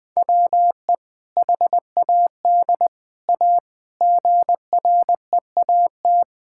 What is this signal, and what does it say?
Morse code, heard as a keyed tone, about 20 wpm, 700 Hz. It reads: WE HAD A GREAT
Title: Sentences with 2-Word Phrases - 20wpm